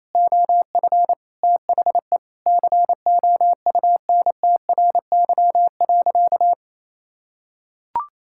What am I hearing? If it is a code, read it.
Morse code, heard as a keyed tone, about 28 wpm, 700 Hz. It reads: OF THE COUNTRY.
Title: Sentences from Top 1000 Words - Review Part 6 - 28wpm